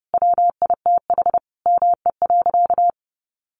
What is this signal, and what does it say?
Morse code, heard as a keyed tone, about 30 wpm, 700 Hz. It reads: WITH ME.
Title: Sentences from Top 500 Words 30wpm